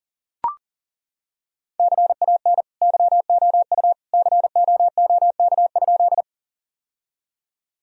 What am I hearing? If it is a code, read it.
Morse code, heard as a keyed tone, about 40 wpm, 700 Hz. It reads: CAN YOU COOK?
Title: Sentences from Top 1000 Words - Review Part 4 - 40wpm